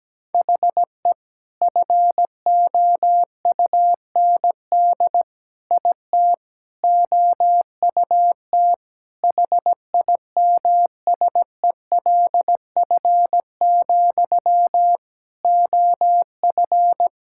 Morse code, heard as a keyed tone, about 17 wpm, 700 Hz. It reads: HE FOUND IT OUT HIMSELF, OF
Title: Sentences from Top 1600 Words 17wpm